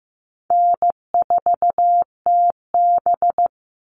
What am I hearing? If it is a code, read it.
Morse code, heard as a keyed tone, about 15 wpm, 700 Hz. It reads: N4TB
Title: Easy Call Signs - 15wpm